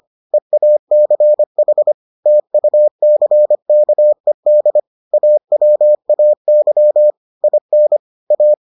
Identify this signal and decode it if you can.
Morse code, heard as a keyed tone, about 25 wpm, 600 Hz. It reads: EACH TUCKED AWAY IN A